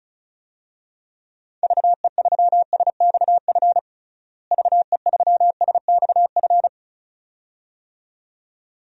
Morse code, heard as a keyed tone, about 36 wpm, 700 Hz. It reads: VE3SXF VE3SXF
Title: DXCC-100 Call Signs - 6 Characters - Repeated 2x - 35wpm